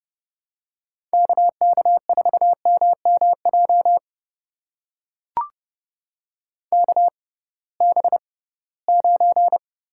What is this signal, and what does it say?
Morse code, heard as a keyed tone, about 30 wpm, spaced as for 19 wpm, 700 Hz. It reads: KK4MMJ KB9
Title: USA Call Signs - 6x Character Spacing - 30wpm